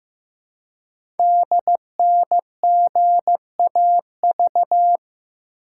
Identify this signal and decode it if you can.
Morse code, heard as a keyed tone, about 15 wpm, 700 Hz. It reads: DNGAV
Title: Code Groups - 5 Letters - 15wpm